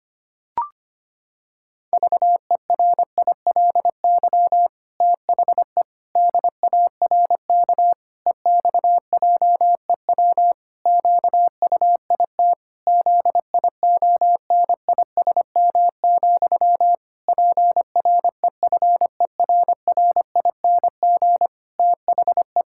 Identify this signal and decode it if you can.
Morse code, heard as a keyed tone, about 25 wpm, 700 Hz. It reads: VERILY THE DARK EXJEW QUIT ZIONISM, PREFERRING THE
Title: Pangram Challenge v2 - 25wpm